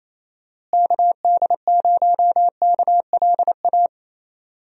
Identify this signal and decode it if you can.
Morse code, heard as a keyed tone, about 27 wpm, 700 Hz. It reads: KD0KLA
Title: USA Call Signs - 6 Characters - Repeated 3x - 28wpm